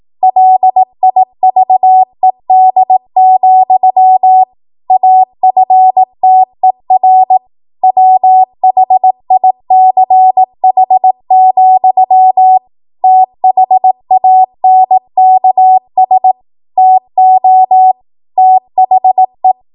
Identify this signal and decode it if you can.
Morse code, heard as a keyed tone, about 18 wpm, 750 Hz. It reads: LIVED, AFTER WHICH, THANKS TO THE
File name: Archive / 18 WPM / 18WPM